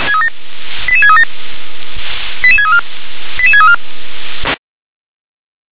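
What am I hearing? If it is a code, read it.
Five-tone selective call (ZVEI), heard as quick runs of five tones, tones of about 70 ms. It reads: 88637 88634 88634
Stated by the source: ZVEI five-tone call 88636, received by SDR.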